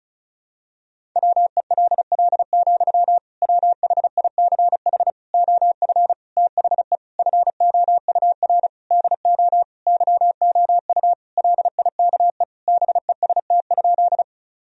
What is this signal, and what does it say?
Morse code, heard as a keyed tone, about 36 wpm, 700 Hz. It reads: WELL, WHICH OF THE FOUR DO YOU LIKE BEST?
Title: Sentences from Top 1000 Words - Review Part 1 - 35wpm